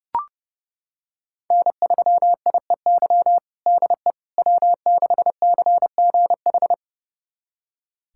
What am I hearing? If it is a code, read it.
Morse code, heard as a keyed tone, about 30 wpm, 700 Hz. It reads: N3IEY DE W6CGH
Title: USA Callsign DE Callsign - 5 Characters - 30wpm